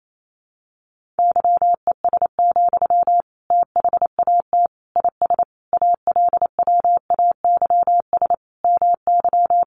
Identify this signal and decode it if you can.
Morse code, heard as a keyed tone, about 28 wpm, 700 Hz. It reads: YES, THAT IS ALWAYS MY
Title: Sentences from Top 1500 Words 28wpm